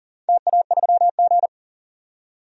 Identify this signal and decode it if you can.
Morse code, heard as a keyed tone, about 40 wpm, 700 Hz. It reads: TA3G
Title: DXCC-100 Call Signs - 4 Characters - 40wpm